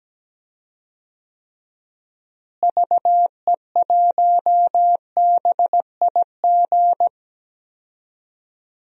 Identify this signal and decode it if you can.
Morse code, heard as a keyed tone, about 17 wpm, 700 Hz. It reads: VE1BIG
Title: DXCC-100 Call Signs - 6 Characters - Repeated 3x - 17wpm